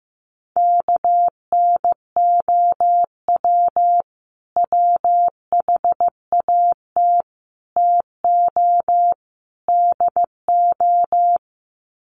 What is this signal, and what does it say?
Morse code, heard as a keyed tone, about 15 wpm, 700 Hz. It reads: KNOW WHAT TO DO
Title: Sets of 4 Words 15wpm